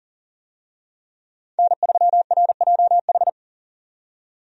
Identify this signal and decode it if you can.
Morse code, heard as a keyed tone, about 39 wpm, 700 Hz. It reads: N3RJH
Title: USA Call Signs - 5 Characters - 40wpm